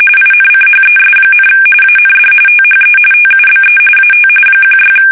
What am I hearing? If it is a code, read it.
RTTY, about 45.45 baud. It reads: INCHES, AND THE TUBE AT 16
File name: fsk850 / battleshipnj